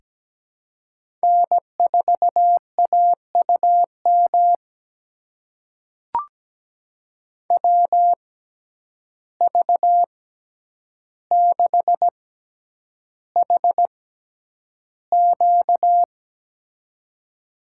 Morse code, heard as a keyed tone, about 17 wpm, spaced as for 9 wpm, 700 Hz. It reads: N4AUM WV6HQ
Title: USA Call Signs - 5 Characters - 6x Character Spacing - 17wpm